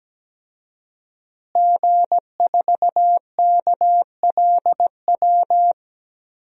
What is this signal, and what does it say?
Morse code, heard as a keyed tone, about 17 wpm, 700 Hz. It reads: G4KLW
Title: DXCC-100 Call Signs - 5 Characters - Repeated 3x - 17wpm